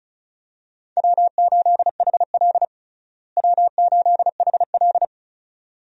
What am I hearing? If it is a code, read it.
Morse code, heard as a keyed tone, about 36 wpm, 700 Hz. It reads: W8HL W8HL
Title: USA Call Signs - 4 Characters - Repeated 2x - 35wpm